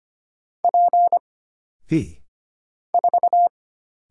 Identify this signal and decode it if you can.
Morse code, heard as a keyed tone, about 25 wpm, 700 Hz. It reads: P 4
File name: Single Character (Letter:Number:Punctuation:Prosign) - Mind-Melt - 25wpm